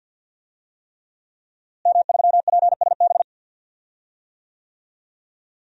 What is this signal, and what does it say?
Morse code, heard as a keyed tone, about 51 wpm, 700 Hz. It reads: M3PSB
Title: DXCC-100 Call Signs - 5 Characters - Repeated 3x - 50wpm